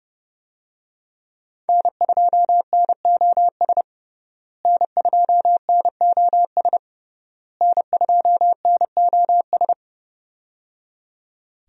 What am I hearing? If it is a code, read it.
Morse code, heard as a keyed tone, about 30 wpm, 700 Hz. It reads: N2NOS N2NOS N2NOS
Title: USA Call Signs - 5 Characters - Repeated 3x - 30wpm